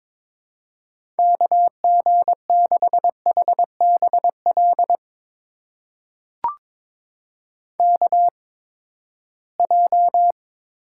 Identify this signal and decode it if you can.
Morse code, heard as a keyed tone, about 22 wpm, spaced as for 15 wpm, 700 Hz. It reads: KG6HBL KJ
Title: USA Call Signs - 6 Characters - 8x Character Spacing - 22wpm